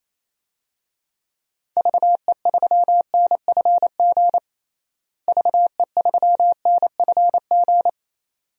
Morse code, heard as a keyed tone, about 28 wpm, 700 Hz. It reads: VE3NFG VE3NFG
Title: DXCC-100 Call Signs - 6 Characters - Repeated 2x - 28wpm